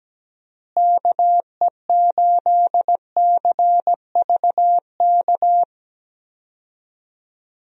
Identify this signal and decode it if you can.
Morse code, heard as a keyed tone, about 17 wpm, 700 Hz. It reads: KE8CVK
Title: USA Call Signs - 6 Characters - Repeated 2x - 17wpm